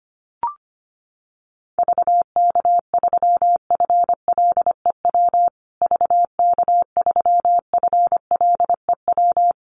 Morse code, heard as a keyed tone, about 25 wpm, 700 Hz. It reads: VK3FLEW VK3FLEW
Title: USA Call Signs Repeated 3x 25wpm